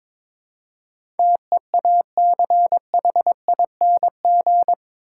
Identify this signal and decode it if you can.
Morse code, heard as a keyed tone, about 22 wpm, 700 Hz. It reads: TEACHING
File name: practice / 22wpm / Eight Letter Words - 22wpm